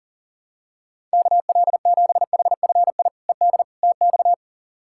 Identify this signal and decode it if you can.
Morse code, heard as a keyed tone, about 37 wpm, 700 Hz. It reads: KL7HFI ED TX
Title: K1USN SST QSO Practice - 40wpm